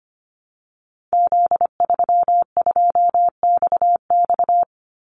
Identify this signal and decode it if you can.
Morse code, heard as a keyed tone, about 25 wpm, 700 Hz. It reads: Z32XX
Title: Call Signs Repeated 2x - 25wpm